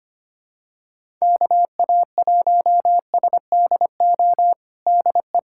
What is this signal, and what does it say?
Morse code, heard as a keyed tone, about 25 wpm, 700 Hz. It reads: KA1SDO DE
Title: USA Callsign DE Callsign - 6 Characters - 25wpm